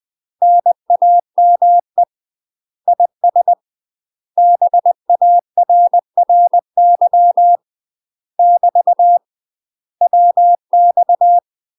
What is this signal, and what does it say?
Morse code, heard as a keyed tone, about 20 wpm, 700 Hz. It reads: NAME IS BARRY <BT> WX
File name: UK MASTER QSOs TRAINING / 20WPM / UK MASTER QSOs 0065